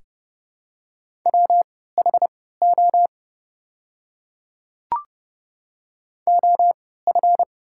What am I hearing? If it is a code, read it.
Morse code, heard as a keyed tone, about 30 wpm, spaced as for 17 wpm, 700 Hz. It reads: WHO OF
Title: 3x Character Spacing - Top 100 Words - 30wpm